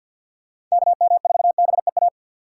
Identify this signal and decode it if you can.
Morse code, heard as a keyed tone, about 52 wpm, 690 Hz. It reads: KM4BEA
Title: USA Call Signs - 2x Character Spacing - 50wpm